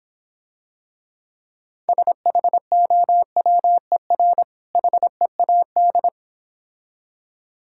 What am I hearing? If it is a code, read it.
Morse code, heard as a keyed tone, about 26 wpm, 700 Hz. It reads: SHOWER HEAD